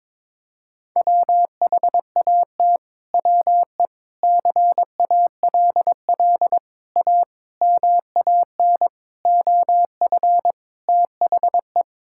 Morse code, heard as a keyed tone, about 22 wpm, 700 Hz. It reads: WHAT WE CALL A MAN OF THE
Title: Sentences from Top 1600 Words 22wpm